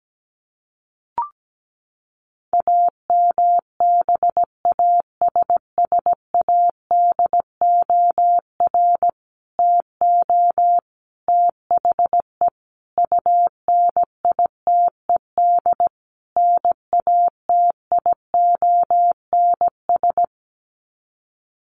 Morse code, heard as a keyed tone, about 17 wpm, 700 Hz. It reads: AMBASSADOR TO THE UNITED NATIONS
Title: Sets of 5 Words Encore2 17wpm